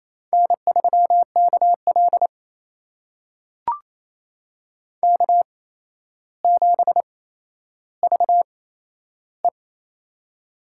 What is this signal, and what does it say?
Morse code, heard as a keyed tone, about 27 wpm, spaced as for 12 wpm, 700 Hz. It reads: N3KL K7VE
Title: USA Call Signs - 4 Characters - 8x Character Spacing - 28wpm